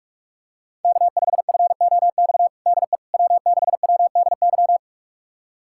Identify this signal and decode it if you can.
Morse code, heard as a keyed tone, about 46 wpm, 700 Hz. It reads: K5FOX DE W6WDY
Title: USA Callsign DE Callsign - 5 Characters - 45wpm